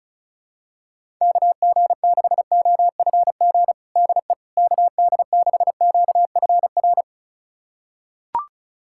Morse code, heard as a keyed tone, about 36 wpm, 700 Hz. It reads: KG6OFG DE KD6QFR
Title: USA Callsign DE Callsign 35wpm